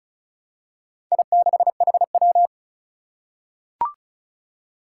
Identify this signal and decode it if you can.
Morse code, heard as a keyed tone, about 36 wpm, 700 Hz. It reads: I6HW
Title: Four Challenging Letter-Number Combinations 35wpm